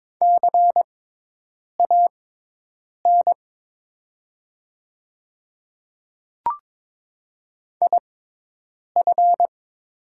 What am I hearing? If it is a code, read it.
Morse code, heard as a keyed tone, about 22 wpm, spaced as for 8 wpm, 700 Hz. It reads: CAN IF
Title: 6x Character Spacing - Top 100 Words - 22wpm